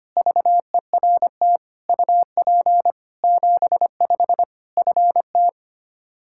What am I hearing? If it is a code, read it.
Morse code, heard as a keyed tone, about 25 wpm, 700 Hz. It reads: VERT UP 75 FT
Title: Practice QSO - Exchange 2 - Info - 25wpm